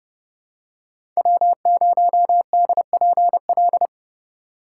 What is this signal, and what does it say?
Morse code, heard as a keyed tone, about 30 wpm, 700 Hz. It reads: W0DPL
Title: USA Call Signs - 5 Characters - 2x Character Spacing - 30wpm